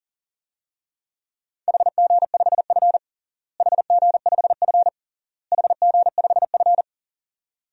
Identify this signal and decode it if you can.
Morse code, heard as a keyed tone, about 40 wpm, 700 Hz. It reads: HG5F HG5F HG5F
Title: DXCC-100 Call Signs - 4 Characters - Repeated 3x - 40wpm